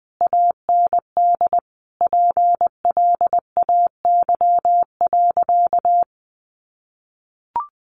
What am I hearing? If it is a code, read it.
Morse code, heard as a keyed tone, about 20 wpm, 700 Hz. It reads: AND PLAY.
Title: Sentences from Top 1300 Words 20wpm